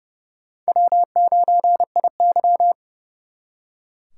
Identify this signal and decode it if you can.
Morse code, heard as a keyed tone, about 37 wpm, 700 Hz. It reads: W9IY
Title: Speed Racing - USA 1x2 Call Signs 30wpm